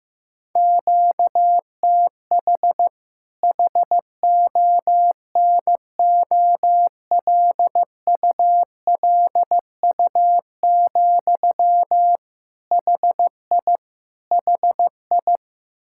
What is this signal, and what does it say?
Morse code, heard as a keyed tone, about 16 wpm, 700 Hz. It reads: QTH HONOLULU, HI HI
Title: Practice QSO - Exchange 1 - Beginning - 15wpm